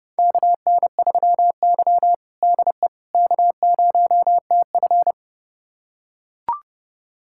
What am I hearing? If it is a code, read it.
Morse code, heard as a keyed tone, about 30 wpm, 700 Hz. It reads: KN3Y DE K0TF
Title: USA Callsign DE Callsign - 4 Characters - 30wpm